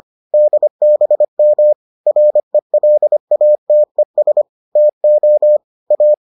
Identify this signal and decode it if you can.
Morse code, heard as a keyed tone, about 25 wpm, 600 Hz. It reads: DBM RELATES TO A